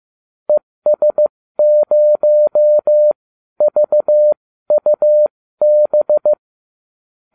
Morse code, heard as a keyed tone, about 15 wpm, spaced as for 12 wpm, 600 Hz. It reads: ES0VUB